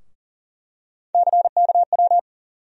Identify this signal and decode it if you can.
Morse code, heard as a keyed tone, about 37 wpm, 700 Hz. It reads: CKW
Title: USA Call Sign Postfixes - 3 Letters 40wpm